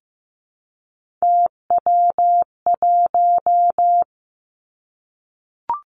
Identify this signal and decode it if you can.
Morse code, heard as a keyed tone, about 15 wpm, 700 Hz. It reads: TW1
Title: Three Challenging Letter-Number Combinations 15wpm